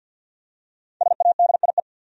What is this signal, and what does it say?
Morse code, heard as a keyed tone, about 46 wpm, 700 Hz. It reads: SADIE
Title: USA Common Names - 50wpm